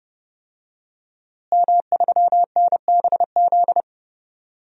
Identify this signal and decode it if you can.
Morse code, heard as a keyed tone, about 30 wpm, 700 Hz. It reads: M3NBZ
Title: DXCC-100 Call Signs - 5 Characters - Repeated 2x - 30wpm